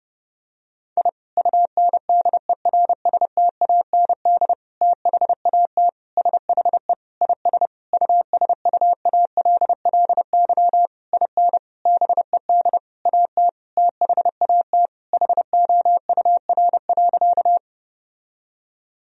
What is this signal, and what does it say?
Morse code, heard as a keyed tone, about 30 wpm, 700 Hz. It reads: I UNDERSTAND THAT SHE IS USUALLY IN BED AT THAT HOUR.
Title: Sentences from Top 1000 Words - Review Part 3 - 30wpm